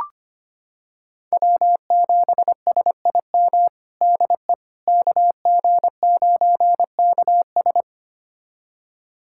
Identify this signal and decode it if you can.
Morse code, heard as a keyed tone, about 25 wpm, 700 Hz. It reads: W7SIM DE KG9KS
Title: USA Callsign DE Callsign - 5 Characters - 25wpm